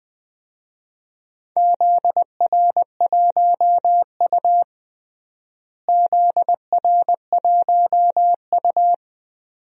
Morse code, heard as a keyed tone, about 21 wpm, 700 Hz. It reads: ZR1U ZR1U
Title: DXCC-100 Call Signs - 4 Characters - Repeated 2x - 20wpm